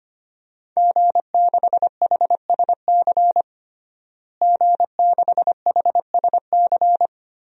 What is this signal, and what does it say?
Morse code, heard as a keyed tone, about 25 wpm, 700 Hz. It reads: G6HSC G6HSC
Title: DXCC-100 Call Signs - 5 Characters - Repeated 2x - 25wpm